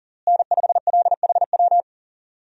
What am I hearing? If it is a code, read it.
Morse code, heard as a keyed tone, about 40 wpm, 700 Hz. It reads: N5LHW
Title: USA Call Signs - 5 Characters - Repeated 3x - 40wpm